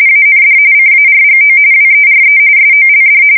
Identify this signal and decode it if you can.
RTTY, about 45.45 baud. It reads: AND CARGO WERE SOLD
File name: fsk170 / usnavy3